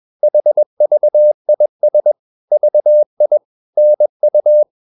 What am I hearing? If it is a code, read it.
Morse code, heard as a keyed tone, about 21 wpm, 600 Hz. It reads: HVIS VI NU
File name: Ferie QRS 21wpm